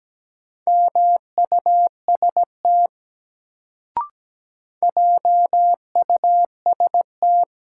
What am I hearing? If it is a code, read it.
Morse code, heard as a keyed tone, about 17 wpm, 700 Hz. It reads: MUST JUST
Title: Sentences from Top 200 Words - New Word Drill - 17wpm